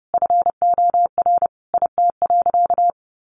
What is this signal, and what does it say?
Morse code, heard as a keyed tone, about 30 wpm, 700 Hz. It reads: FOR IT.
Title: Sentences from Top 1100 Words 30wpm